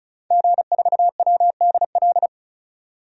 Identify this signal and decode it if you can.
Morse code, heard as a keyed tone, about 36 wpm, 700 Hz. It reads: G4WDL
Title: DXCC-100 Call Signs - 5 Characters - 35wpm